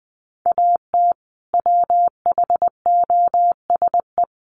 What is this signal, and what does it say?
Morse code, heard as a keyed tone, about 20 wpm, 700 Hz. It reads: AT WHOSE
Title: Sentences from Top 1200 Words 20wpm